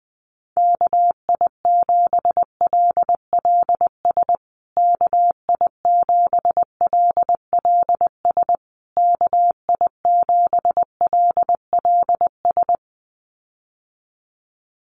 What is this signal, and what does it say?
Morse code, heard as a keyed tone, about 20 wpm, 700 Hz. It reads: KI7LLS KI7LLS KI7LLS
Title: USA Call Signs Repeated 3x 20wpm